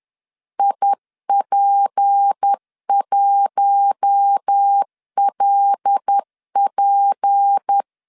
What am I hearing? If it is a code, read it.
Morse code, heard as a keyed tone, about 10 wpm, 800 Hz. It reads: IP1LP
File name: LEZIONI / 6 PKX0 / 6 P 10wpm 2432